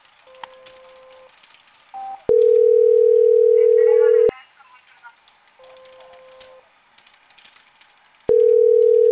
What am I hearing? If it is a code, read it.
Ringback tone (North America), 440 + 480 Hz.